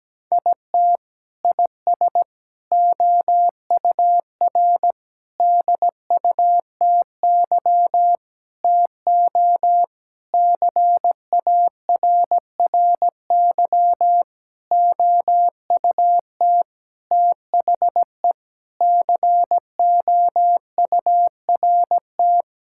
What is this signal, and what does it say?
Morse code, heard as a keyed tone, about 17 wpm, 700 Hz. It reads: IT IS OUR DUTY TO CARRY OUT THE COURT
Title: Sentences from Top 1600 Words 17wpm